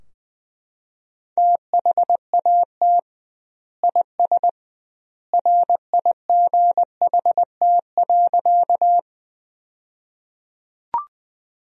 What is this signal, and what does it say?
Morse code, heard as a keyed tone, about 21 wpm, 700 Hz. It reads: THAT IS RIGHT.
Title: 2x Word-Spacing - Sentences from Top 200 Words - 20wpm